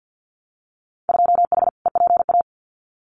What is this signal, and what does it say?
Morse code, heard as a keyed tone, about 50 wpm, 700 Hz. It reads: 2H EPA
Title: Winter Field Day Exchanges 50wpm